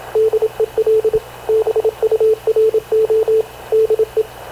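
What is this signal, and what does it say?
Morse code, heard as a keyed tone, about 27 wpm, 430 Hz, with radio noise. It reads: DEL BURO DE
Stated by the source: Havana Radio CLA press (PX) broadcast, recorded by Radio Officer Paul Dumesnil aboard mv CNAV QUEST/CZDO.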